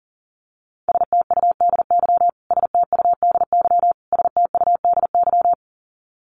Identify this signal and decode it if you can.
Morse code, heard as a keyed tone, about 40 wpm, 700 Hz. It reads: STUDY STUDY STUDY
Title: Top 500 Words Repeated 40wpm